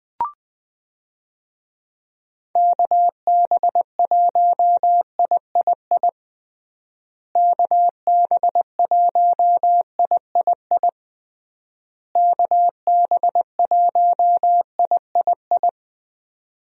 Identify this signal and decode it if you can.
Morse code, heard as a keyed tone, about 21 wpm, 700 Hz. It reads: KB1III KB1III KB1III
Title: USA Call Signs - 6 Characters - Repeated 3x - 20wpm